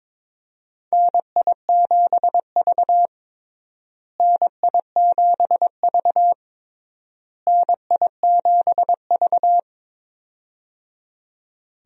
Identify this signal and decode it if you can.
Morse code, heard as a keyed tone, about 22 wpm, 700 Hz. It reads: NI7V NI7V NI7V
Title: USA Call Signs - 4 Characters - Repeated 3x - 22wpm